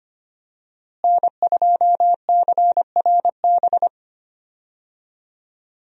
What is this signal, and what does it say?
Morse code, heard as a keyed tone, about 25 wpm, 690 Hz. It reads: N2CRB
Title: USA Call Signs - 5 Characters - Repeated 2x - 25wpm